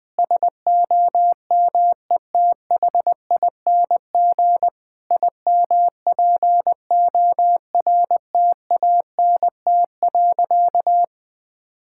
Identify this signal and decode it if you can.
Morse code, heard as a keyed tone, about 20 wpm, 700 Hz. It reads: SOMETHING IMPORTANT.
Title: Sentences with 5-Word Phrases - 20wpm